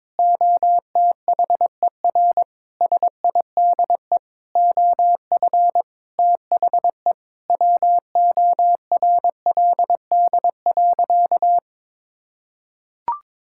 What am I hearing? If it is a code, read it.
Morse code, heard as a keyed tone, about 21 wpm, 700 Hz. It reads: OTHER SIDE OF THE WORLD.
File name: Sentences with 7-Word Phrases - 22wpm